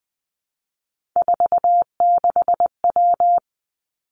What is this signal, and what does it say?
Morse code, heard as a keyed tone, about 20 wpm, 700 Hz. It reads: 46W
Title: Three Challenging Letter-Number Combinations 20wpm